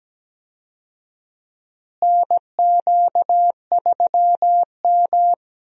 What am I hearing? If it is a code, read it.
Morse code, heard as a keyed tone, about 17 wpm, 700 Hz. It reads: NQ3M
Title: USA Call Signs - 4 Characters - Repeated 2x - 17wpm